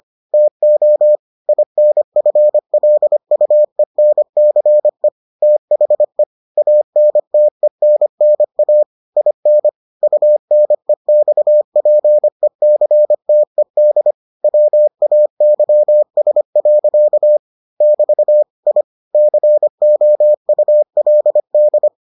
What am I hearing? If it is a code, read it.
Morse code, heard as a keyed tone, about 25 wpm, 600 Hz. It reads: TO INFLUENCE THE ANTENNA IN UNEXPECTED WAYS. <BT> I COULD